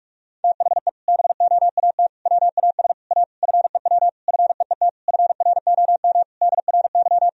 Morse code, heard as a keyed tone, about 46 wpm, 700 Hz. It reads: THE BOAT WAS A FEW FEET FROM DRY
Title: Sentences from Top 1000 Words - Review Part 3 - 45wpm